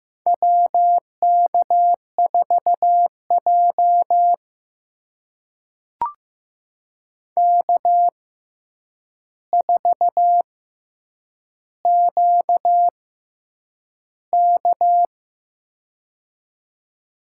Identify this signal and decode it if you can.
Morse code, heard as a keyed tone, about 15 wpm, spaced as for 8 wpm, 700 Hz. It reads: WK4J K4QK
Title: USA Call Signs - 4 Characters - 6x Character Spacing - 15wpm